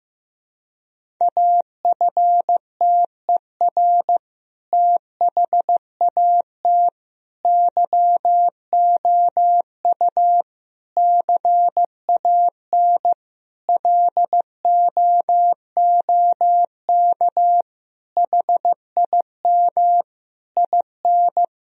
Morse code, heard as a keyed tone, about 15 wpm, 700 Hz. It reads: AFTER THAT YOU CAN LOOK HIM IN
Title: Sentences from Top 1000 Words - Review Part 2 - 15wpm